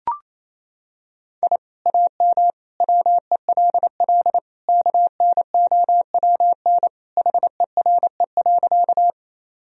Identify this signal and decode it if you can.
Morse code, heard as a keyed tone, about 28 wpm, 700 Hz. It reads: I AM WELL KNOWN HERE.
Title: Sentences from Top 1000 Words - Review Part 3 - 28wpm